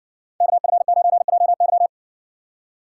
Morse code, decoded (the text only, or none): KF9JY